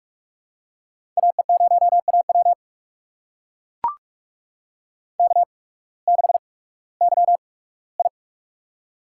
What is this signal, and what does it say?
Morse code, heard as a keyed tone, about 44 wpm, spaced as for 21 wpm, 700 Hz. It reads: AE0AW K6YI
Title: USA Call Signs - 8x Character Spacing - 45wpm